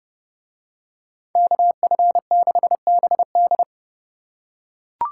Morse code, heard as a keyed tone, about 29 wpm, 700 Hz. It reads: KF6BD